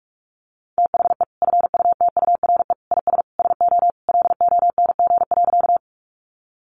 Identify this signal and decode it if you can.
Morse code, heard as a keyed tone, about 44 wpm, 700 Hz. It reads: THE FUTURE IS SO LONG.
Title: Sentences from Top 700 Words 45wpm